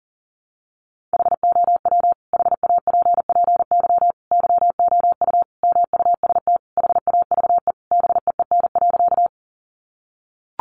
Morse code, heard as a keyed tone, about 40 wpm, 700 Hz. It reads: HOW HAPPY YOU MUST HAVE BEEN.
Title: Sentences from Top 1000 Words 40wpm